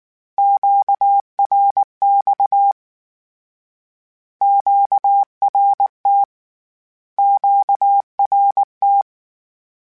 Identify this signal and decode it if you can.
Morse code, heard as a keyed tone, about 19 wpm, 800 Hz. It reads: QRX QRT QRT
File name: Q Calls Twice @18